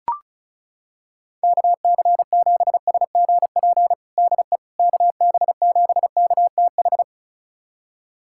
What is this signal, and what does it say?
Morse code, heard as a keyed tone, about 36 wpm, 700 Hz. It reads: KC7SGP DE KB7KTH
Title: USA Callsign DE Callsign 35wpm